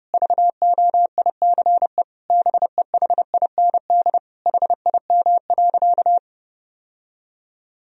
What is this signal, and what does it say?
Morse code, heard as a keyed tone, about 32 wpm, 700 Hz. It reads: VOICE BEHIND HIM.
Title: Sentences from Top 1000 Words - Review Part 1 - 30wpm